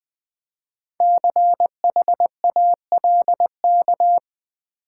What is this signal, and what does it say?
Morse code, heard as a keyed tone, about 21 wpm, 690 Hz. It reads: CHALK
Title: Easy Words with Hints - 20wpm